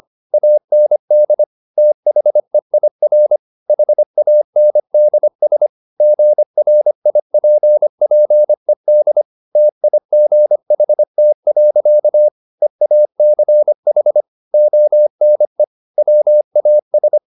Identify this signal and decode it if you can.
Morse code, heard as a keyed tone, about 25 wpm, 600 Hz. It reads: AND THEIR HANDS GRIPPED TIGHT. EACH ONE WAS
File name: the old man and the sea part 1 25wpm